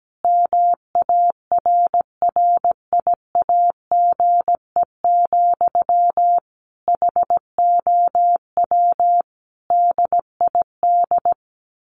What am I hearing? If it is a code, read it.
Morse code, heard as a keyed tone, about 17 wpm, 700 Hz. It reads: MARRIAGE, HOW DID
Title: Sentences from Top 1100 Words 17wpm